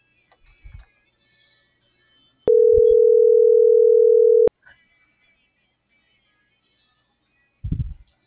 Ringback tone (North America), 440 + 480 Hz.
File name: CENTRAL MABE / out celeste.alcantara